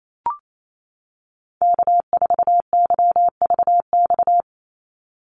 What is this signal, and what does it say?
Morse code, heard as a keyed tone, about 28 wpm, 700 Hz. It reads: K4YVX
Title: USA Call Signs 28wpm